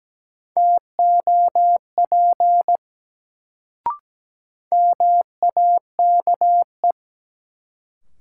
Morse code, heard as a keyed tone, about 17 wpm, 700 Hz. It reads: TOP MAKE
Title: Sets of 2 Words - New Word Drill - 17wpm